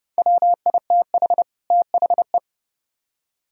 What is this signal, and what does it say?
Morse code, heard as a keyed tone, about 38 wpm, 700 Hz. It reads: WITH THE